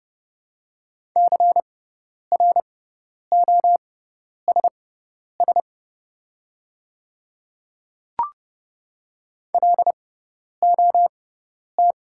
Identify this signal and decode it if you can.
Morse code, heard as a keyed tone, about 30 wpm, spaced as for 10 wpm, 700 Hz. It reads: CROSS LOT
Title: 6x Character Spacing - Top 500 Words - 30wpm